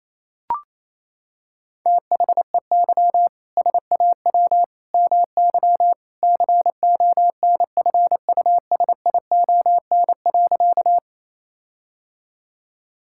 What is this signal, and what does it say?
Morse code, heard as a keyed tone, about 28 wpm, 700 Hz. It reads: THEY SAW MY CONFUSION.
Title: Sentences from Top 1600 Words 28wpm